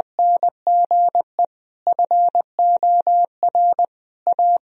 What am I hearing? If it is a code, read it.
Morse code, heard as a keyed tone, about 20 wpm, 700 Hz. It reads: NGE FOR A
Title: Sentences from Top 1000 Words - Review Part 2 - 20wpm